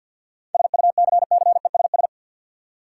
Morse code, heard as a keyed tone, about 50 wpm, 700 Hz. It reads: SUCCESS
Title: Seven Letter Words - 50wpm